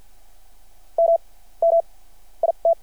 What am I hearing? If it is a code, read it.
Morse code, heard as a keyed tone, about 53 wpm, spaced as for 18 wpm, 720 Hz. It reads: MMIT
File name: archived / CRSnet